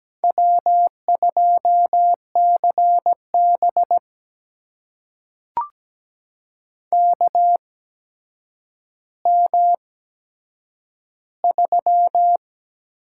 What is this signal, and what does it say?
Morse code, heard as a keyed tone, about 17 wpm, spaced as for 8 wpm, 700 Hz. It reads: W2CB KM3
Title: USA Call Signs - 8x Character Spacing - 17wpm